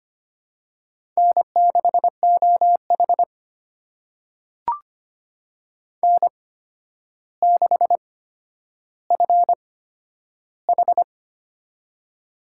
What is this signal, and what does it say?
Morse code, heard as a keyed tone, about 25 wpm, spaced as for 11 wpm, 700 Hz. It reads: N6OH N6FH
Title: USA Call Signs - 8x Character Spacing - 25wpm